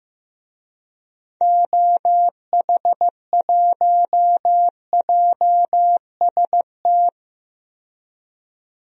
Morse code, heard as a keyed tone, about 15 wpm, 700 Hz. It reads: OH1JST
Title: DXCC-100 Call Signs - 6 Characters - 15wpm